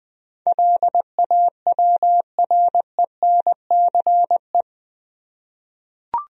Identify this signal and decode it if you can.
Morse code, heard as a keyed tone, about 20 wpm, 700 Hz. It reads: LAWRENCE
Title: USA Common Names - 20wpm